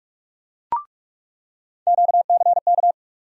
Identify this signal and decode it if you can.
Morse code, heard as a keyed tone, about 42 wpm, 700 Hz. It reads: QCK
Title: USA Call Sign Postfixes - 3 Letters 45wpm